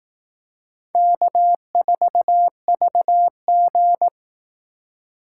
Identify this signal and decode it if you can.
Morse code, heard as a keyed tone, about 18 wpm, 700 Hz. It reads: K4VG
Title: Speed Racing - USA Call Signs 15wpm